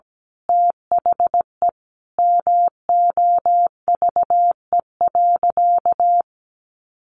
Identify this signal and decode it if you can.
Morse code, heard as a keyed tone, about 17 wpm, 700 Hz. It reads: THE MOVE.
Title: Sentences from Top 1400 Words 17wpm